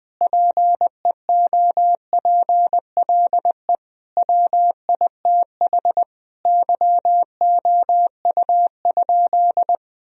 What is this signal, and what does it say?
Morse code, heard as a keyed tone, about 20 wpm, 700 Hz. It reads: PEOPLE WITH YOU?
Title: Sentences from Top 1000 Words - Review Part 1 - 20wpm